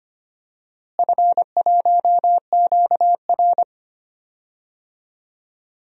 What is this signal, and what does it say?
Morse code, heard as a keyed tone, about 25 wpm, 700 Hz. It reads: F1QR
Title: DXCC-100 Call Signs - 4 Characters - Repeated 2x - 25wpm